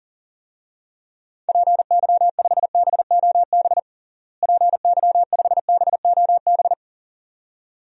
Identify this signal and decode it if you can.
Morse code, heard as a keyed tone, about 40 wpm, 700 Hz. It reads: PY5BOB PY5BOB
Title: DXCC-100 Call Signs - 6 Characters - Repeated 2x - 40wpm